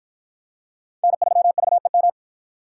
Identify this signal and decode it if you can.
Morse code, heard as a keyed tone, about 53 wpm, 700 Hz. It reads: N3VEM